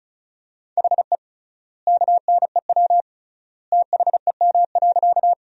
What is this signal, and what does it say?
Morse code, heard as a keyed tone, about 34 wpm, 700 Hz. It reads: HE KNEW THEM.
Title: 3x Word-Spacing - Sentences from Top 400 Words - 35wpm